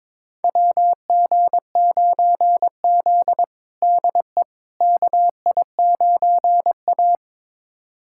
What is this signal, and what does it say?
Morse code, heard as a keyed tone, about 22 wpm, 700 Hz. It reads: WG9Z DE KI9A
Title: USA Callsign DE Callsign - 4 Characters - 22wpm